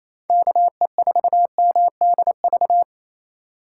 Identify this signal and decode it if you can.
Morse code, heard as a keyed tone, about 33 wpm, 700 Hz. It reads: KE4MDV
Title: Speed Racing - USA 2x3 Call Signs 28wpm